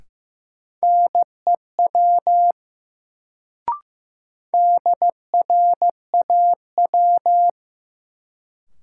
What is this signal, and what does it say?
Morse code, heard as a keyed tone, about 15 wpm, 700 Hz. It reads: NEW DRAW
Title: Sentences from Top 300 Words - New Word Drill - 15wpm